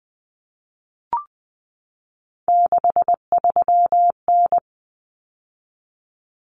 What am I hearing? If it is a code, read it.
Morse code, heard as a keyed tone, about 20 wpm, 700 Hz. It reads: 63N
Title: Common Cut Numbers - 3-Digit 20wpm